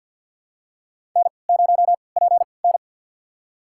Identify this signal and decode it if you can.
Morse code, heard as a keyed tone, about 51 wpm, spaced as for 28 wpm, 700 Hz. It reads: N0PN
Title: USA Call Signs - 4 Characters - 3x Character Spacing - 50wpm